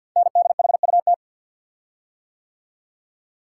Morse code, heard as a keyed tone, about 54 wpm, 700 Hz. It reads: NDHUT
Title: Code Groups - 5 Letters - 50wpm